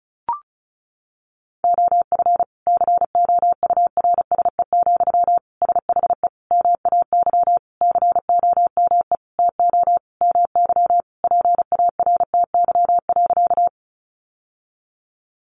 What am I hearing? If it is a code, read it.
Morse code, heard as a keyed tone, about 34 wpm, 700 Hz. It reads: OF COURSE, SHE MAY COME TO MY PARTY.
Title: Sentences from Top 1400 Words 35wpm